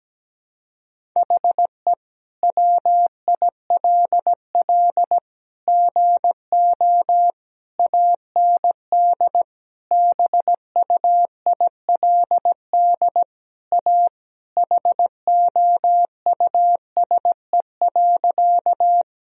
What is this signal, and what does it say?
Morse code, heard as a keyed tone, about 17 wpm, 700 Hz. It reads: HE WILL GO AND BUILD A HOUSE.
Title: Sentences from Top 1000 Words - Review Part 1 - 17wpm